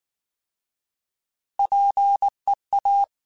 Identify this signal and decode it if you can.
Morse code, heard as a keyed tone, about 19 wpm, 800 Hz. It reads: PEA
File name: Alphabetical / Alphabetical mp3s / Ps @18